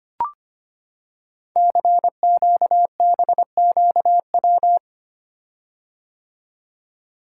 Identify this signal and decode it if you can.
Morse code, heard as a keyed tone, about 25 wpm, 690 Hz. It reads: CQBQW
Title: Code Groups - 5 Letter:Numbers - 25wpm